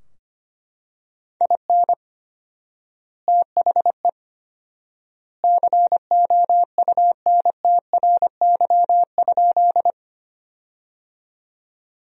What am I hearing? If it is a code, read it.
Morse code, heard as a keyed tone, about 25 wpm, 700 Hz. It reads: IN THE COUNTRY?
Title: 4x Word-Spacing - Sentences from Top 300 Words - 25wpm